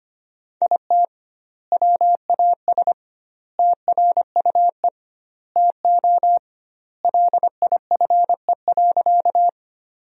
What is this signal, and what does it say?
Morse code, heard as a keyed tone, about 25 wpm, 700 Hz. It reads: IT WAS TRUE TO LIFE.
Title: 2x Word-Spacing - Sentences from Top 400 Words - 25wpm